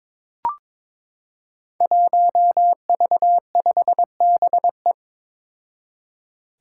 Morse code, heard as a keyed tone, about 22 wpm, 700 Hz. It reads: 1V5BE
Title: Five Challenging Letter-Number Combinations 22wpm